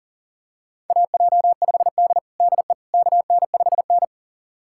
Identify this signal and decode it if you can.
Morse code, heard as a keyed tone, about 40 wpm, 700 Hz. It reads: AJ5D DE KN5N
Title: USA Callsign DE Callsign - 4 Characters - 40wpm